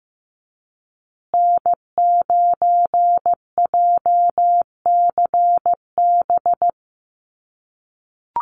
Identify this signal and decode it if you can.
Morse code, heard as a keyed tone, about 15 wpm, 700 Hz. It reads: N9JCB
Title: USA Call Signs Repeated 2x 15wpm